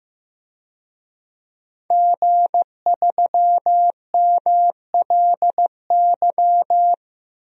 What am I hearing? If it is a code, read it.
Morse code, heard as a keyed tone, about 15 wpm, 700 Hz. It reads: G3MLY
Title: DXCC-100 Call Signs - 5 Characters - 15wpm